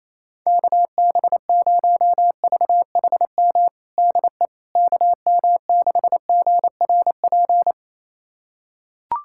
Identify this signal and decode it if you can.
Morse code, heard as a keyed tone, about 28 wpm, 700 Hz. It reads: KB0VHM DE KM6GRP
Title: USA Callsign DE Callsign - 6 Characters - 28wpm